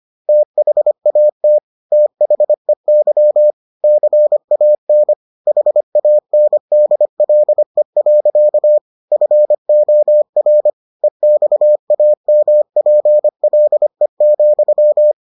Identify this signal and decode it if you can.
Morse code, heard as a keyed tone, about 25 wpm, 600 Hz. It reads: THAT THEY CAN HANDLE. FOR EXAMPLE,